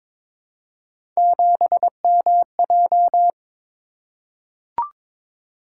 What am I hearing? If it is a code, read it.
Morse code, heard as a keyed tone, about 22 wpm, 700 Hz. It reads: 7MJ
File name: practice / 22wpm / Three Letter-Number Combinations - 22wpm